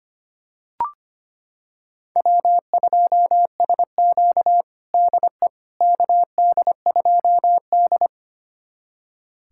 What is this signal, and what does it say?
Morse code, heard as a keyed tone, about 25 wpm, 700 Hz. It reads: W2SQ DE KD2D
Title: USA Callsign DE Callsign - 4 Characters - 25wpm